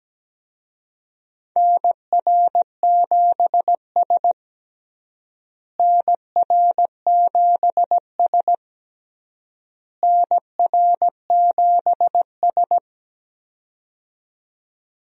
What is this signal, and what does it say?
Morse code, heard as a keyed tone, about 17 wpm, 700 Hz. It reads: NR7S NR7S NR7S
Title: USA Call Signs - 4 Characters - Repeated 3x - 17wpm